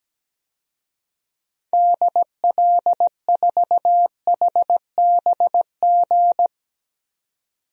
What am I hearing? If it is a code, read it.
Morse code, heard as a keyed tone, about 17 wpm, 700 Hz. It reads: DL4HBG
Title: DXCC-100 Call Signs - 6 Characters - 17wpm